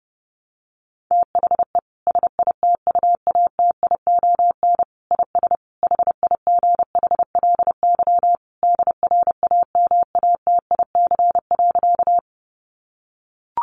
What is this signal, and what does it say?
Morse code, heard as a keyed tone, about 28 wpm, 700 Hz. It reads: THE SITUATION IS HIGHLY DRAMATIC.
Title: Sentences from Top 1300 Words 30wpm